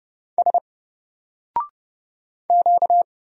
Morse code, heard as a keyed tone, about 30 wpm, 700 Hz. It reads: S Q
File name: Single-Letter-Number-30wpm